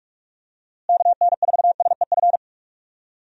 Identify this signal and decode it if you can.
Morse code, heard as a keyed tone, about 46 wpm, 700 Hz. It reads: KN4SEF
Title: USA Call Signs - 6 Characters - Repeated 2x - 45wpm